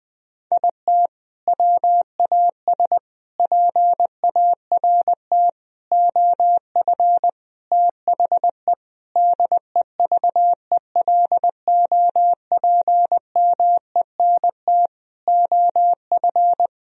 Morse code, heard as a keyed tone, about 20 wpm, 700 Hz. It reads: IT WAS PART OF THE DEVELOPMENT OF
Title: Sentences from Top 1000 Words - Review Part 4 - 20wpm